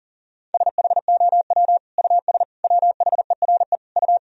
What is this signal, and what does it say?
Morse code, heard as a keyed tone, about 40 wpm, 700 Hz. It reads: SHOW US WHERE U
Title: Sentences from Top 200 Words and Abbreviations 40wpm